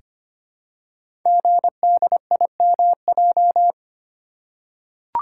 Morse code, heard as a keyed tone, about 25 wpm, 700 Hz. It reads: GDIMJ
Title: Code Groups - 5 Letters - 25wpm